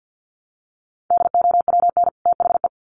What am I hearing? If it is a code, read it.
Morse code, heard as a keyed tone, about 48 wpm, 700 Hz. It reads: DOWN THE
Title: Sets of 2 Words 50wpm